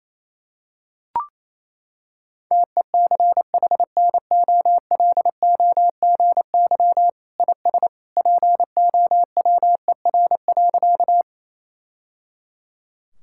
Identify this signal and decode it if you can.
Morse code, heard as a keyed tone, about 28 wpm, 700 Hz. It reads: TECHNOLOGY IS POWER.
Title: Sentences from Top 1000 Words - Review Part 5 - 28wpm